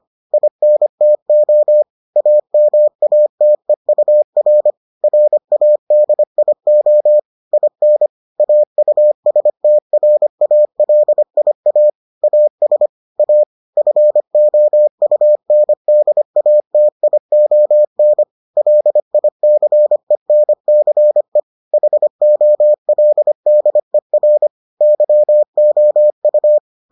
Morse code, heard as a keyed tone, about 25 wpm, 600 Hz. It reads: INTO AMATEUR RADIO IN AUSTRALIA AS A FOUNDATION LICENCE HOLDER YOU